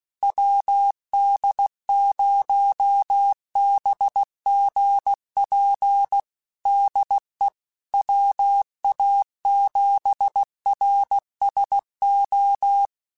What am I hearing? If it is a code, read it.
Morse code, heard as a keyed tone, about 16 wpm, 800 Hz. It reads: WD0BGP DE WA7RSO
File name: QSOs / QSOs-mp3s / WD0BGP 1@15